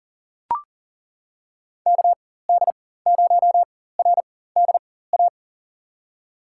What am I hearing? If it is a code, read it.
Morse code, heard as a keyed tone, about 39 wpm, spaced as for 18 wpm, 700 Hz. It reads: KD0RDA
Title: USA Call Signs - 6 Characters - 4x Character Spacing - 40wpm